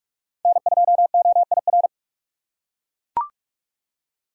Morse code, heard as a keyed tone, about 41 wpm, 700 Hz. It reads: N2OIR
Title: USA Call Signs - 3x Character Spacing - 45wpm